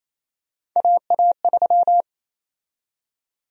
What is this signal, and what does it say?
Morse code, heard as a keyed tone, about 28 wpm, 700 Hz. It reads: AA3
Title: USA Call Sign Prefixes - Repeated 2x - 28wpm